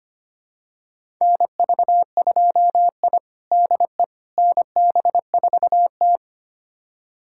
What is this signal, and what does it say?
Morse code, heard as a keyed tone, about 25 wpm, 700 Hz. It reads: NV2I DE NB4T
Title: USA Callsign DE Callsign - 4 Characters - 25wpm